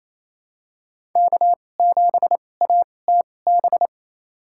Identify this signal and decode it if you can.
Morse code, heard as a keyed tone, about 28 wpm, spaced as for 20 wpm, 700 Hz. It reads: K7ATB
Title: USA Call Signs - 2x Character Spacing - 28wpm